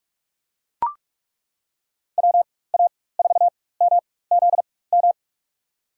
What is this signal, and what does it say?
Morse code, heard as a keyed tone, about 42 wpm, spaced as for 21 wpm, 700 Hz. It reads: WA4MZM
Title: USA Call Signs - 6 Characters - 4x Character Spacing - 45wpm